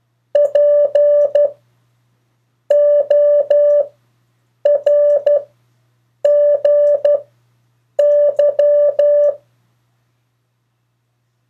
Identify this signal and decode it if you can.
Morse code, heard as a keyed tone, about 12 wpm, spaced as for 7 wpm, 580 Hz. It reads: PORGY
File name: 2of4-words-1